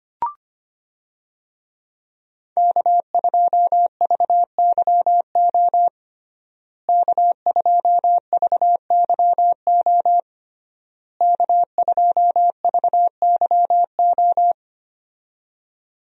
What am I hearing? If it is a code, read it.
Morse code, heard as a keyed tone, about 25 wpm, 700 Hz. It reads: K2VYO K2VYO K2VYO
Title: USA Call Signs - 5 Characters - Repeated 3x - 25wpm